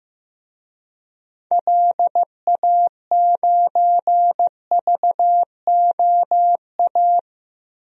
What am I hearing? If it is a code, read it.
Morse code, heard as a keyed tone, about 15 wpm, 700 Hz. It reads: LA9VOA
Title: DXCC-100 Call Signs - 6 Characters - 15wpm